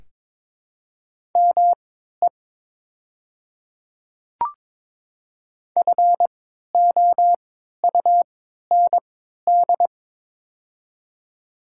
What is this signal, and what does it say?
Morse code, heard as a keyed tone, about 22 wpm, spaced as for 12 wpm, 700 Hz. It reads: ME FOUND
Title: 3x Character Spacing - Top 300 Words - 22wpm